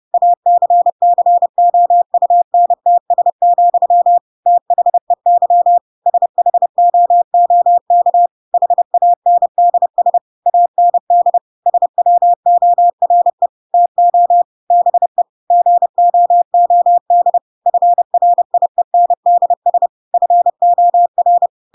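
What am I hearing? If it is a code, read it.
Morse code, heard as a keyed tone, about 30 wpm, 700 Hz. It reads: ACCOUNTS, THEY SHOOK HANDS AND SWORE TO BE GOOD FRIENDS FOR
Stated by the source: Chapter 2 of The Adventures of Pinocchio read as a book in Morse code (CW) at 30 wpm, by ik1hge.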